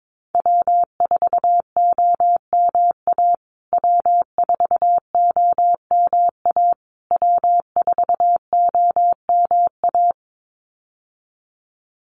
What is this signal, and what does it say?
Morse code, heard as a keyed tone, about 22 wpm, 700 Hz. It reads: W4OMA W4OMA W4OMA
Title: USA Call Signs Repeated 3x 22wpm